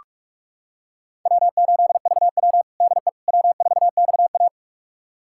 Morse code, heard as a keyed tone, about 46 wpm, 700 Hz. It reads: W8VW DE W4XA
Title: USA Callsign DE Callsign - 4 Characters - 45wpm